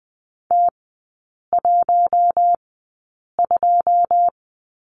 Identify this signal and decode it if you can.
Morse code, heard as a keyed tone, about 20 wpm, spaced as for 8 wpm, 690 Hz. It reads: T12
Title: 081 Lesson 14e - Number 2 - Three Characters 20wpm